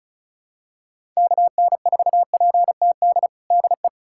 Morse code, heard as a keyed tone, about 34 wpm, 700 Hz. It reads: KN4PTD DE
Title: USA Callsign DE Callsign - 6 Characters - 35wpm